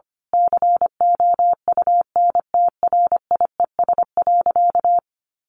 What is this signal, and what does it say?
Morse code, heard as a keyed tone, about 25 wpm, 700 Hz. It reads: COUNTRIES.
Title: A Sound Of Thunder 25wpm